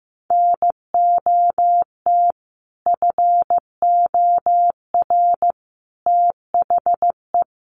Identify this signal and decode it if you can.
Morse code, heard as a keyed tone, about 15 wpm, 700 Hz. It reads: NOT FOR THE
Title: Sentences from Top 600 Words 15wpm